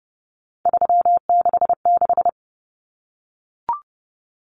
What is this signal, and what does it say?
Morse code, heard as a keyed tone, about 30 wpm, 700 Hz. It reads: 366
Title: Common Cut Numbers - 3-Digit 30wpm